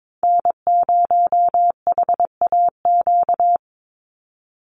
Morse code, heard as a keyed tone, about 22 wpm, 700 Hz. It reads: N0HAQ
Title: USA Call Signs 22wpm